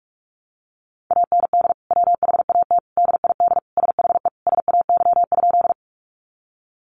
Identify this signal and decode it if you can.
Morse code, heard as a keyed tone, about 44 wpm, 700 Hz. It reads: AND WHAT DID SHE SAY?
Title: Sentences from Top 200 Words 45wpm